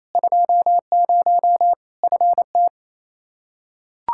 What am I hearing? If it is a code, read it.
Morse code, heard as a keyed tone, about 28 wpm, 700 Hz. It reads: 20 FT
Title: Practice QSO - Exchange 2 - Info - 28wpm